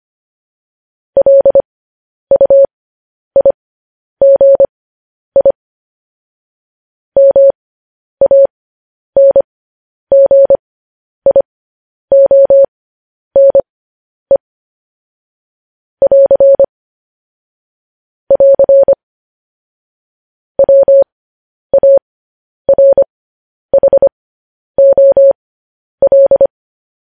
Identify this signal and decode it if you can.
Morse code, heard as a keyed tone, about 25 wpm, spaced as for 10 wpm, 550 Hz. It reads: LUIGI MANGIONE <AR> <AR> WARHOL
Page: news headlines in morse code at 10 wpm